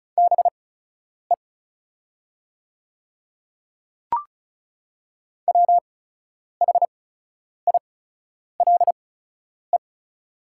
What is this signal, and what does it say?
Morse code, heard as a keyed tone, about 34 wpm, spaced as for 10 wpm, 700 Hz. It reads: BE WHILE
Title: 8x Character Spacing - Top 300 Words - 35wpm